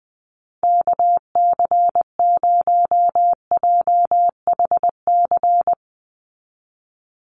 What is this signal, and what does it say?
Morse code, heard as a keyed tone, about 20 wpm, 700 Hz. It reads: KC0JHC
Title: USA Call Signs 20wpm